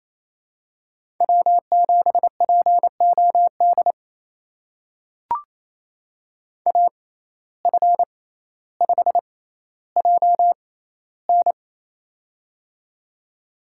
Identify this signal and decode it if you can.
Morse code, heard as a keyed tone, about 28 wpm, spaced as for 14 wpm, 700 Hz. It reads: W7POD AF5JN
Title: USA Call Signs - 5 Characters - 6x Character Spacing - 28wpm